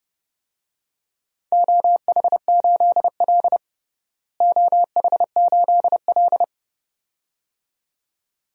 Morse code, heard as a keyed tone, about 30 wpm, 700 Hz. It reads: OH8L OH8L
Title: DXCC-100 Call Signs - 4 Characters - Repeated 2x - 30wpm